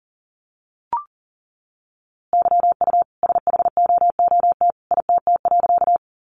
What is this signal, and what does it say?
Morse code, heard as a keyed tone, about 40 wpm, 700 Hz. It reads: YU SHOOT ITT.
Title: A Sound Of Thunder 40wpm